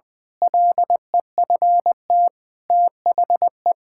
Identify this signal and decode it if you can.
Morse code, heard as a keyed tone, about 20 wpm, 700 Hz. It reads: LEFT THE
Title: Pangram Challenge v2 - 20wpm